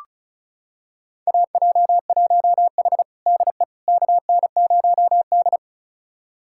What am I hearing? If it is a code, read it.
Morse code, heard as a keyed tone, about 36 wpm, 700 Hz. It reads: AJ1H DE KN0D
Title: USA Callsign DE Callsign - 4 Characters - 35wpm